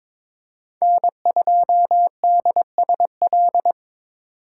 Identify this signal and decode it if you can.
Morse code, heard as a keyed tone, about 22 wpm, 700 Hz. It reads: N2DSL
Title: USA Call Signs - 4x Character Spacing - 22wpm